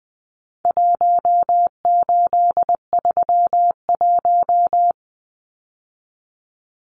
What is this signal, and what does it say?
Morse code, heard as a keyed tone, about 20 wpm, 700 Hz. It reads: 1831
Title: Numbers - Contest Style - 4-Digit 20wpm